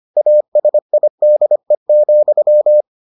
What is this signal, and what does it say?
Morse code, heard as a keyed tone, about 25 wpm, 600 Hz. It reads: ASIDE,